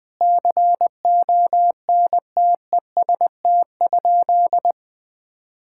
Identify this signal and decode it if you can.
Morse code, heard as a keyed tone, about 20 wpm, 700 Hz. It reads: CONTEST?
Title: Sentences from Top 1600 Words 20wpm